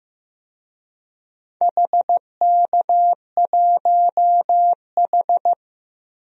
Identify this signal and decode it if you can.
Morse code, heard as a keyed tone, about 15 wpm, 700 Hz. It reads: HK1H
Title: DXCC-100 Call Signs - 4 Characters - 15wpm